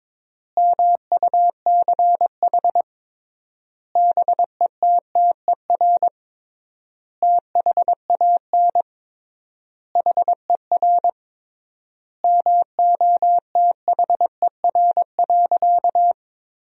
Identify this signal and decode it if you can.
Morse code, heard as a keyed tone, about 22 wpm, 700 Hz. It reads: MUCH BETTER THAN HER MOTHER.
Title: 3x Word-Spacing - Sentences from Top 400 Words - 22wpm